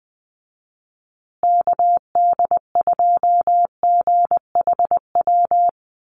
Morse code, heard as a keyed tone, about 20 wpm, 700 Hz. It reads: KD2GHW
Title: USA Call Signs Repeated 3x 20wpm